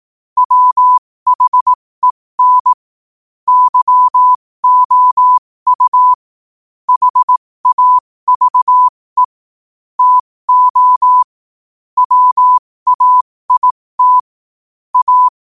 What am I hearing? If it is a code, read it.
Morse code, heard as a keyed tone, about 18 wpm, spaced as for 15 wpm, 1000 Hz. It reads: WHEN YOU HAVE TO WAIT A
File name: qotdfiles / 15wpm